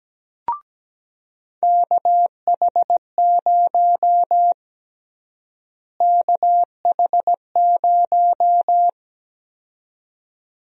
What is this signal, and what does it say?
Morse code, heard as a keyed tone, about 17 wpm, 700 Hz. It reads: KH0 KH0
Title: USA Call Sign Prefixes - Repeated 2x - 17wpm